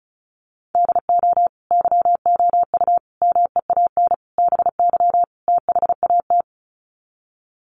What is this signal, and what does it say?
Morse code, heard as a keyed tone, about 34 wpm, 700 Hz. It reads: DO YOU MEAN BY THAT
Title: Sets of 5 Words Encore2 35wpm